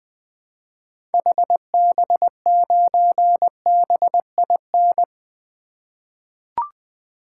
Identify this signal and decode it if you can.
Morse code, heard as a keyed tone, about 20 wpm, 700 Hz. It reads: HB9BIN
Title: 2023 SOTA Activator Call Signs - 20wpm